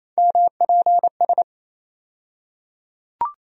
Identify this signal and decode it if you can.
Morse code, heard as a keyed tone, about 27 wpm, 700 Hz. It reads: MPS
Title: USA Call Sign Postfixes - 3 Letters 28wpm